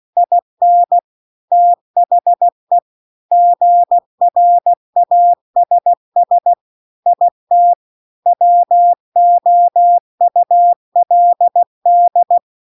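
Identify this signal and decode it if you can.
Morse code, heard as a keyed tone, about 16 wpm, 700 Hz. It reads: IN THE GRASS IT WOULD
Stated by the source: Morse practice text at 16 WPM.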